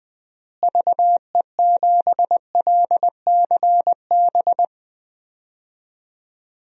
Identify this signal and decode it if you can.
Morse code, heard as a keyed tone, about 20 wpm, 700 Hz. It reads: VE7LCB
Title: DXCC-100 Call Signs - 6 Characters - Repeated 3x - 20wpm